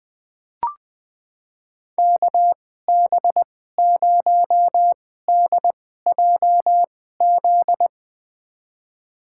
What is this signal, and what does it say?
Morse code, heard as a keyed tone, about 21 wpm, spaced as for 15 wpm, 700 Hz. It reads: KB0DJZ
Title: USA Call Signs - 2x Character Spacing - 20wpm